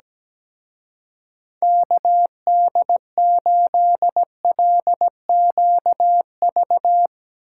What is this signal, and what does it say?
Morse code, heard as a keyed tone, about 17 wpm, 700 Hz. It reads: KD8LQV
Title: Speed Racing - USA 2x3 Call Signs 17wpm